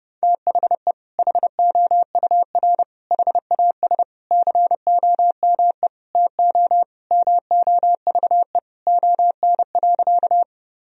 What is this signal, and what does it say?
Morse code, heard as a keyed tone, about 32 wpm, 700 Hz. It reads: THE HOUR HAS COME TO MOVE ON.
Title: Sentences from Top 1000 Words - Review Part 1 - 30wpm